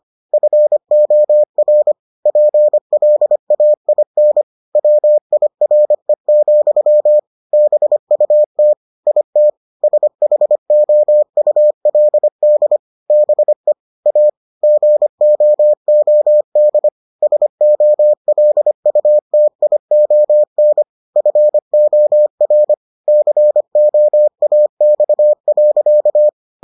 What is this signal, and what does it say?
Morse code, heard as a keyed tone, about 25 wpm, 600 Hz. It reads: FOR PLAIN WIRE, BUT IT SHOULD BE A GOOD SOLUTION FOR COAX.